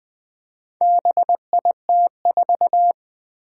Morse code, heard as a keyed tone, about 20 wpm, 700 Hz. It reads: BIT4
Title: Four Challenging Letter-Number Combinations 20wpm